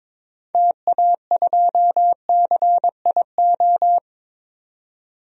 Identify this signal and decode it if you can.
Morse code, heard as a keyed tone, about 22 wpm, 700 Hz. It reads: TA2CIO
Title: DXCC-100 Call Signs - 6 Characters - 22wpm